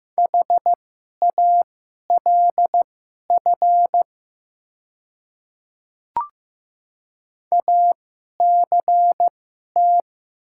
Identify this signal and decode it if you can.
Morse code, heard as a keyed tone, about 15 wpm, spaced as for 11 wpm, 700 Hz. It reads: HALF ACT
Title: 2x Character Spacing - Top 500 Words - 15wpm